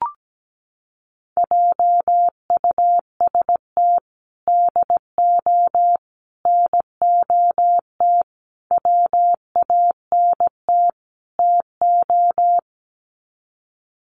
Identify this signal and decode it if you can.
Morse code, heard as a keyed tone, about 17 wpm, 700 Hz. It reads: JUST DO NOT WANT TO
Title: Sets of 5 Words Encore 17wpm